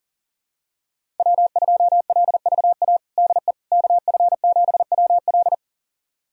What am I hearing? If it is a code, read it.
Morse code, heard as a keyed tone, about 40 wpm, 700 Hz. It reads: W2LVA DE KF7WL
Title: USA Callsign DE Callsign - 5 Characters - 40wpm